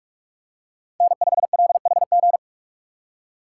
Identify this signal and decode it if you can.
Morse code, heard as a keyed tone, about 44 wpm, 690 Hz. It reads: N5LHG
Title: USA Call Signs - 5 Characters - 4x Character Spacing - 45wpm